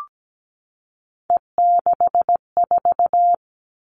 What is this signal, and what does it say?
Morse code, heard as a keyed tone, about 17 wpm, 700 Hz. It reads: E64
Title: All Cut Numbers - 3-Digit 17wpm